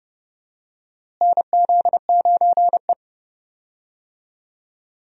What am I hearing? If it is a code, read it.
Morse code, heard as a keyed tone, about 30 wpm, 700 Hz. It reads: NZ9E
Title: USA Call Signs - 4 Characters - Repeated 3x - 30wpm